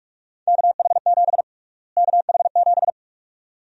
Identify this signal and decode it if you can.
Morse code, heard as a keyed tone, about 42 wpm, 700 Hz. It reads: KH7 KH7
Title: USA Call Sign Prefixes - Repeated 2x - 45wpm